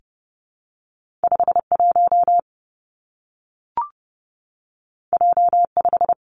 Morse code, heard as a keyed tone, about 30 wpm, 700 Hz. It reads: 5 1 J 5
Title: Two Challenging Letter-Number Combinations 30wpm